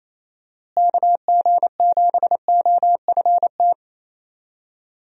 Morse code, heard as a keyed tone, about 28 wpm, 700 Hz. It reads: KG7OFT
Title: USA Call Signs - 6 Characters - Repeated 3x - 28wpm